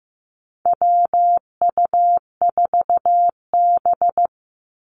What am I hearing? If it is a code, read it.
Morse code, heard as a keyed tone, about 15 wpm, 700 Hz. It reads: WU4B
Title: Call Signs Repeated 2x - 15wpm